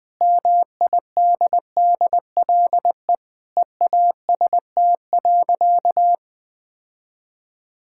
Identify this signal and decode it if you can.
Morse code, heard as a keyed tone, about 20 wpm, 700 Hz. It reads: MIDDLE EAST.
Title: Sentences with 3-Word Phrases - 20wpm